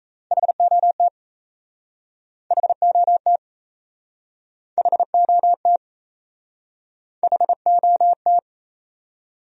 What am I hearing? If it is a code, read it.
Morse code, heard as a keyed tone, about 37 wpm, 700 Hz. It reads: HOT HOT HOT HOT